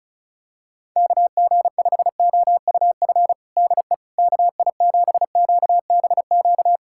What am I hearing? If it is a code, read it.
Morse code, heard as a keyed tone, about 34 wpm, 700 Hz. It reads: KG5OUF DE KI7QBQ
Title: USA Callsign DE Callsign 35wpm